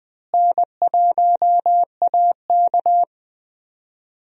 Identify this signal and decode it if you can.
Morse code, heard as a keyed tone, about 21 wpm, 700 Hz. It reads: N1AK
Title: USA Call Signs - 4 Characters - Repeated 3x - 20wpm